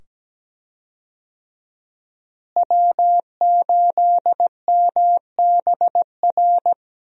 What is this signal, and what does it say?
Morse code, heard as a keyed tone, about 17 wpm, 700 Hz. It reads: W8MBR
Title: USA Call Signs - 5 Characters - Repeated 2x - 17wpm